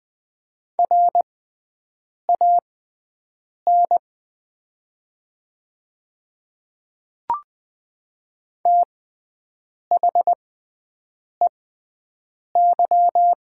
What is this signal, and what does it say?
Morse code, heard as a keyed tone, about 20 wpm, spaced as for 7 wpm, 700 Hz. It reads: RAN THEY
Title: 6x Character Spacing - Top 500 Words - 20wpm